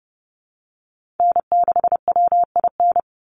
Morse code, heard as a keyed tone, about 30 wpm, 700 Hz. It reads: N6WIN
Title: Call Signs Repeated 2x - 30wpm